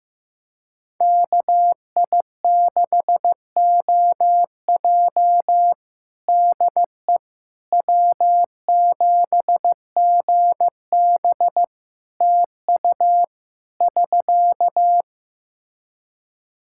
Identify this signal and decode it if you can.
Morse code, heard as a keyed tone, about 14 wpm, 700 Hz. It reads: KI6OJ DE W7GB TU <SK>
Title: Practice QSO - Exchange 4 - Ending - 15wpm